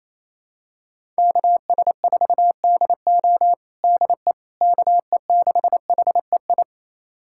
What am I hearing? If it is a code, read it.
Morse code, heard as a keyed tone, about 28 wpm, 700 Hz. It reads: KS4DO DE KE6HEI
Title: USA Callsign DE Callsign 28wpm